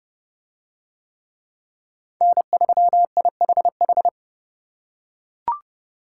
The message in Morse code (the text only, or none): N3IHH